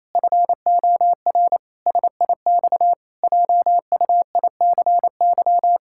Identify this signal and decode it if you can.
Morse code, heard as a keyed tone, about 28 wpm, 700 Hz. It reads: FOR SIX JUICY
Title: Pangram Challenge v2 - 28wpm